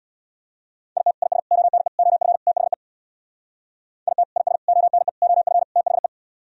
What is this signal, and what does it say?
Morse code, heard as a keyed tone, about 66 wpm, 690 Hz. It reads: IT IS ONE OF THE IT IS ONE OF THE